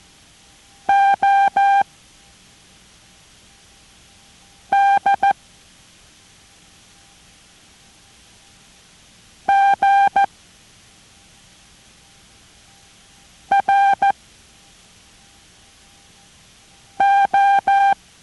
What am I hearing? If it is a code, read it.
Morse code, heard as a keyed tone, about 14 wpm, 790 Hz. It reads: O D G R O